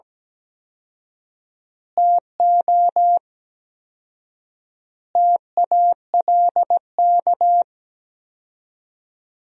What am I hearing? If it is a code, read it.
Morse code, heard as a keyed tone, about 17 wpm, 700 Hz. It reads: TO TALK
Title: 4x Word-Spacing - Sentences from Top 400 Words - 17wpm